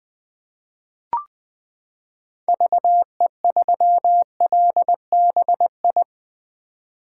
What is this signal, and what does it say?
Morse code, heard as a keyed tone, about 20 wpm, 700 Hz. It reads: VE3LBI
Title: 2023 POTA Hunter Call Signs - 20wpm